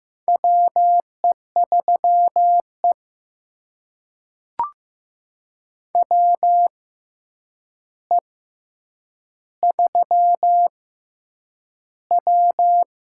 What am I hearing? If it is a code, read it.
Morse code, heard as a keyed tone, about 15 wpm, spaced as for 8 wpm, 700 Hz. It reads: WE3E WE3W
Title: USA Call Signs - 4 Characters - 6x Character Spacing - 15wpm